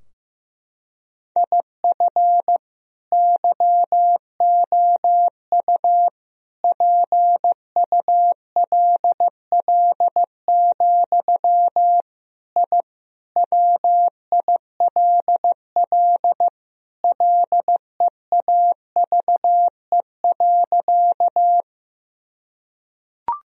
Morse code, heard as a keyed tone, about 15 wpm, 700 Hz. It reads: IF YOU PULL, I WILL LEAVE.
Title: Sentences from Top 1000 Words - Review Part 2 - 15wpm